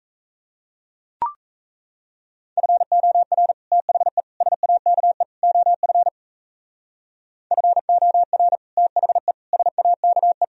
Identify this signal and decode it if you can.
Morse code, heard as a keyed tone, about 35 wpm, 700 Hz. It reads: FOR THE SAKE OF FOR THE SAKE
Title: Speed-Racing - Sets of 4 Words 28wpm